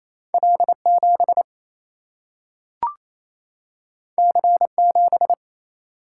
Morse code, heard as a keyed tone, about 28 wpm, 700 Hz. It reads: L 7 C 7
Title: Single-Letter Single-Number 28wpm